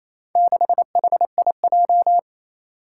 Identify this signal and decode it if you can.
Morse code, heard as a keyed tone, about 27 wpm, 700 Hz. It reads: 6HIJ
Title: Four Challenging Letter-Number Combinations 28wpm